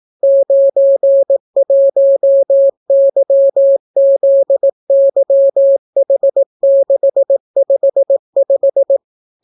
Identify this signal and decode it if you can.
Morse code, heard as a keyed tone, about 18 wpm, 550 Hz. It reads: 91YZYH655